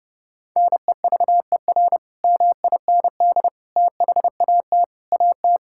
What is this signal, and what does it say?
Morse code, heard as a keyed tone, about 30 wpm, 700 Hz. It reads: NEVER MIND THAT AT
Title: Sentences from Top 1000 Words - Review Part 5 - 30wpm